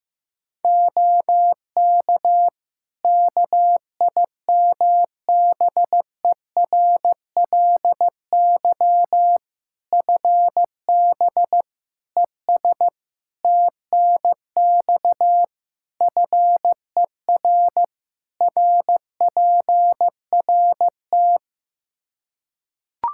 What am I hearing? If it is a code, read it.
Morse code, heard as a keyed tone, about 15 wpm, 700 Hz. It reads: OK KIMBERLY FB ES TNX FER RPRT
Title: Practice QSO - Exchange 2 - Info - 15wpm